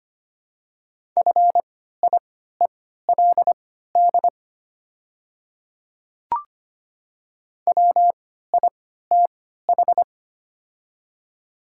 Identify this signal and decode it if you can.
Morse code, heard as a keyed tone, about 25 wpm, spaced as for 14 wpm, 690 Hz. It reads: FIELD WITH
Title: 3x Character Spacing - Top 500 Words - 25wpm